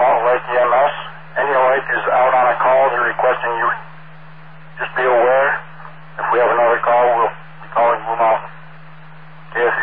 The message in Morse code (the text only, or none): none